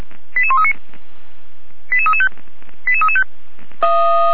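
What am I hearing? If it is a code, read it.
Five-tone selective call (ZVEI), heard as quick runs of five tones, tones of about 70 ms. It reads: 88139 88335 88335
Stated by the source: ZVEI five-tone selective call received on SDR.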